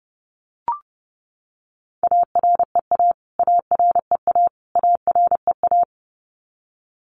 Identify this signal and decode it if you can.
Morse code, heard as a keyed tone, about 30 wpm, 700 Hz. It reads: AREA AREA AREA
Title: Top 500 Words Repeated 30wpm